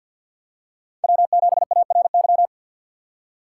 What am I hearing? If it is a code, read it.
Morse code, heard as a keyed tone, about 53 wpm, 700 Hz. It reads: W7ARY
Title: USA Call Signs - 5 Characters - 3x Character Spacing - 50wpm